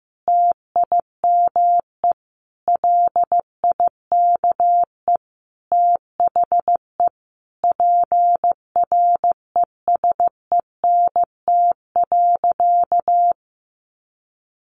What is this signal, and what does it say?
Morse code, heard as a keyed tone, about 15 wpm, 700 Hz. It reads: TIME LIKE THE PRESENT.
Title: Sentences from Top 900 Words 15wpm